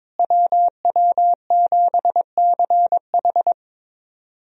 Morse code, heard as a keyed tone, about 22 wpm, 700 Hz. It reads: WW7CH
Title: USA Call Signs - 5 Characters - Repeated 2x - 22wpm